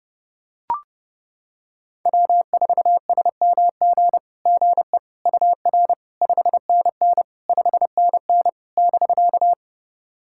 Morse code, heard as a keyed tone, about 30 wpm, 700 Hz. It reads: W4SMG GE UR 5NN 5NN <BK>
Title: SOTA QSO Practice 30wpm